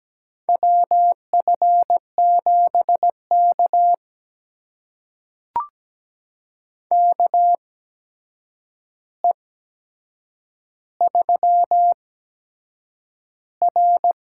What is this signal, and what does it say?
Morse code, heard as a keyed tone, about 17 wpm, spaced as for 7 wpm, 700 Hz. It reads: WF7K KE3R
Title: USA Call Signs - 4 Characters - 8x Character Spacing - 17wpm